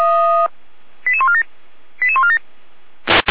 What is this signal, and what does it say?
Five-tone selective call (ZVEI), heard as quick runs of five tones, tones of about 70 ms. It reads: 88147 88147